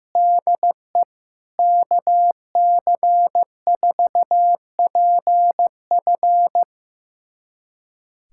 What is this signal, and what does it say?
Morse code, heard as a keyed tone, about 16 wpm, 700 Hz. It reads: DE KC4PF
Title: USA Callsign DE Callsign - 5 Characters - 15wpm